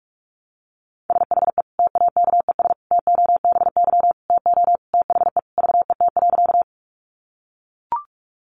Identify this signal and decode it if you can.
Morse code, heard as a keyed tone, about 44 wpm, 700 Hz. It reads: SHE TAKES TOBY TO THE VET.
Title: Easy Phrases 45wpm